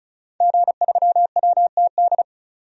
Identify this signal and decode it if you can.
Morse code, heard as a keyed tone, about 36 wpm, 700 Hz. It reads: G3WTD
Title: DXCC-100 Call Signs - 5 Characters - 35wpm